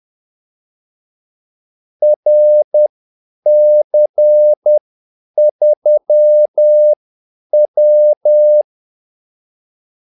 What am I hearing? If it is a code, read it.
Morse code, heard as a keyed tone, about 10 wpm, spaced as for 8 wpm, 600 Hz. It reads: RC3W
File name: Callsigns-10-8wpm